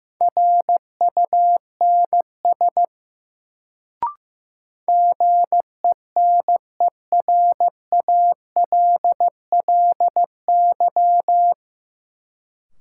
Morse code, heard as a keyed tone, about 15 wpm, 700 Hz. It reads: RUNS GENERALLY
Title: Sentences from Top 600 Words - New Word Drill - 15wpm